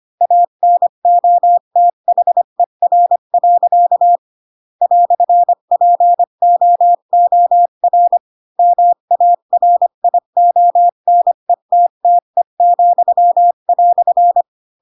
Morse code, heard as a keyed tone, about 25 wpm, 700 Hz. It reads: ANOTHER. "POOR MARIONETTE,"
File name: Pinocchio / Pinocchio ch3 20 25 ik1hge